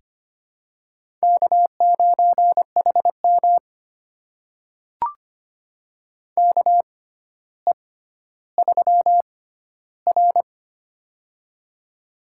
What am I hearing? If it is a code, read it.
Morse code, heard as a keyed tone, about 25 wpm, spaced as for 13 wpm, 700 Hz. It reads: K9HM KE3R
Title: USA Call Signs - 4 Characters - 6x Character Spacing - 25wpm